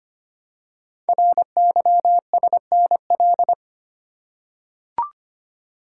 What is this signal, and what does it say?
Morse code, heard as a keyed tone, about 25 wpm, 700 Hz. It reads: RYSNL
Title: Code Groups - 5 Letters - 25wpm